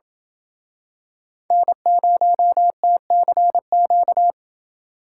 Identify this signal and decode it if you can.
Morse code, heard as a keyed tone, about 27 wpm, 700 Hz. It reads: N0TCQ